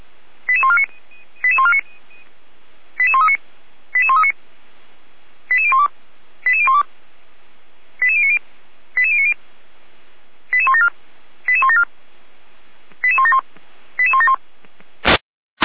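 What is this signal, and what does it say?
Five-tone selective call (ZVEI), heard as quick runs of five tones, tones of about 70 ms. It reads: 88149 88139 88139 88012 88012 88090 88090 88174 88174 88171 88171
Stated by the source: ZVEI five-tone selective call received by SDR, from the feuerwehr-orlamuende.org fire brigade archive.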